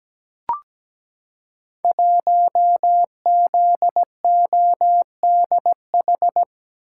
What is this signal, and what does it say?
Morse code, heard as a keyed tone, about 17 wpm, 700 Hz. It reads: 1ZODH
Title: Code Groups - 5 Letter:Numbers - 17wpm